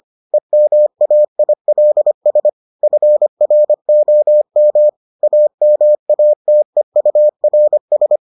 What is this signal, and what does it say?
Morse code, heard as a keyed tone, about 25 wpm, 600 Hz. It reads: EMAILS FROM AMATEURS